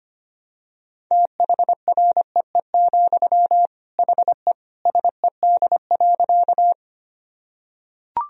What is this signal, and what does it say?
Morse code, heard as a keyed tone, about 25 wpm, 700 Hz. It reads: THREE, HE SED.
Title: Sentences from Top 200 Words and Abbreviations 25wpm